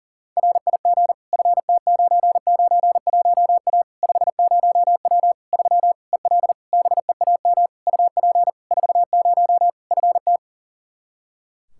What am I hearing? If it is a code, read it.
Morse code, heard as a keyed tone, about 40 wpm, 700 Hz. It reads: RIG FT991A 50W 3 EL BEAM UP 40 FT
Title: Practice QSO - Exchange 2 - Info - 40wpm